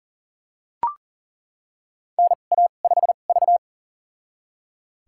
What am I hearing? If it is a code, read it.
Morse code, heard as a keyed tone, about 38 wpm, spaced as for 28 wpm, 690 Hz. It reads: NA5V
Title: USA Call Signs - 4 Characters - 2x Character Spacing - 40wpm